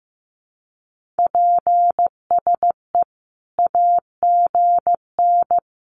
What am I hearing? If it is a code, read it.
Morse code, heard as a keyed tone, about 15 wpm, 700 Hz. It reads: PSE AGN
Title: Common QSO Phrases 15wpm